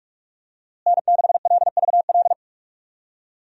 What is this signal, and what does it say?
Morse code, heard as a keyed tone, about 46 wpm, 700 Hz. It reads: N6LVL
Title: USA Call Signs - 5 Characters - Repeated 2x - 45wpm